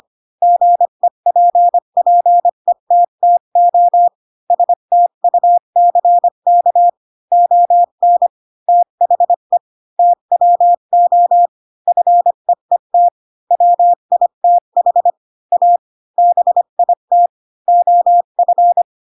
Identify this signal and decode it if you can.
Morse code, heard as a keyed tone, about 25 wpm, 700 Hz. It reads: GEPPETTO STUCK ON THE TWO FEET WITH A BIT OF
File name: Pinocchio / Pinocchio ch8 20 25 ik1hge